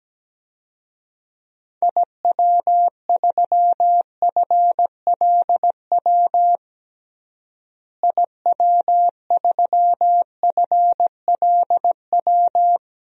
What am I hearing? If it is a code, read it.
Morse code, heard as a keyed tone, about 17 wpm, 700 Hz. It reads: IW3FLW IW3FLW
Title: DXCC-100 Call Signs - 6 Characters - Repeated 3x - 17wpm